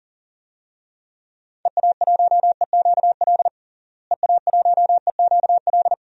Morse code, heard as a keyed tone, about 37 wpm, 700 Hz. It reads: EA1EQL EA1EQL
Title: DXCC-100 Call Signs - 6 Characters - Repeated 2x - 40wpm